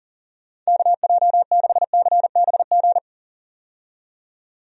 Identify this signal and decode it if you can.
Morse code, heard as a keyed tone, about 40 wpm, 700 Hz. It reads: KJ6CBG
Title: USA Call Signs - 6 Characters - 40wpm